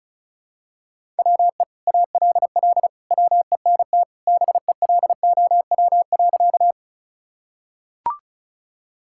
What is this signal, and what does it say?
Morse code, heard as a keyed tone, about 36 wpm, 700 Hz. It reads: WE ALL WENT BELOW.
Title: Sentences from Top 1000 Words - Review Part 5 - 35wpm